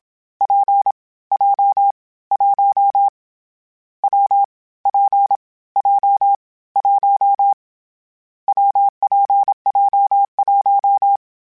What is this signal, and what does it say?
Morse code, heard as a keyed tone, about 27 wpm, 800 Hz. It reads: P J 1 W P J 1 WPJ1
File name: Variable Speed / Variable Speed-mp3s / WPJ WPJ1 sp @25